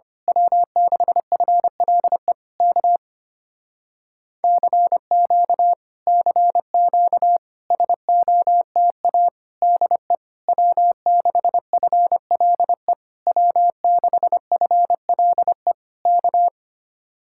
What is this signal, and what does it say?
Morse code, heard as a keyed tone, about 24 wpm, 700 Hz. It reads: W6FLE K CQ CQ SOTA DE W6FLE W6FLE K
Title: Speed-Racing Common QSO Phrases - 25wpm